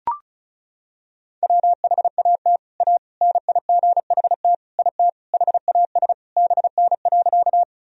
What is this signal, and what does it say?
Morse code, heard as a keyed tone, about 36 wpm, 700 Hz. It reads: WHAT A NIGHT IT HAS BN.
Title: Sentences from Top 300 Words and Abbreviations 35wpm